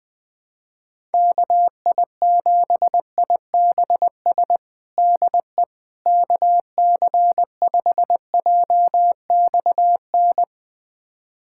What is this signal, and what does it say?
Morse code, heard as a keyed tone, about 20 wpm, 700 Hz. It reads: KI7IBS DE KC5JXN
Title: USA Callsign DE Callsign 20wpm